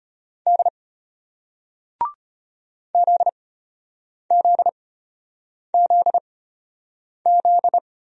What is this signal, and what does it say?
Morse code, heard as a keyed tone, about 32 wpm, spaced as for 8 wpm, 700 Hz. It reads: D ZZZZ